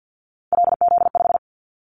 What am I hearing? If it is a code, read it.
Morse code, heard as a keyed tone, about 50 wpm, 700 Hz. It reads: LZ5
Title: Callsign Prefixes for Top 50 DXCC - 50wpm